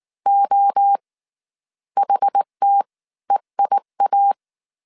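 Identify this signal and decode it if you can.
Morse code, heard as a keyed tone, about 19 wpm, 800 Hz. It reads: O HT EIA
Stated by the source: Morse code dictation at 18 wpm from the IW3SGT QRS Morse code training course, five-character groups of 300 characters.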